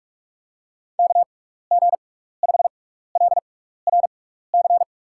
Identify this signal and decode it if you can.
Morse code, heard as a keyed tone, about 43 wpm, spaced as for 15 wpm, 700 Hz. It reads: KG5LRC
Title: USA Call Signs - 6 Characters - 6x Character Spacing - 45wpm